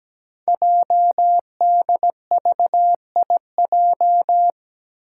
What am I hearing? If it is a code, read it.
Morse code, heard as a keyed tone, about 17 wpm, 700 Hz. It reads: JDVIJ
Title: Code Groups - 5 Letter:Numbers - 17wpm